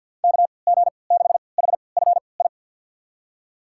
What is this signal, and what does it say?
Morse code, heard as a keyed tone, about 49 wpm, spaced as for 28 wpm, 700 Hz. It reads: KG6HFI
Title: USA Call Signs - 6 Characters - 3x Character Spacing - 50wpm